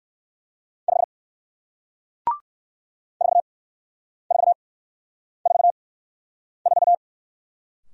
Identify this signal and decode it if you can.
Morse code, heard as a keyed tone, about 54 wpm, spaced as for 9 wpm, 700 Hz. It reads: 5 4444
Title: Speed-Racing Single Letter-Number 45wpm